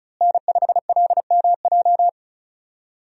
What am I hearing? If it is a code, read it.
Morse code, heard as a keyed tone, about 36 wpm, 700 Hz. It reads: N5LMJ
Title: USA Call Signs - 5 Characters - Repeated 3x - 35wpm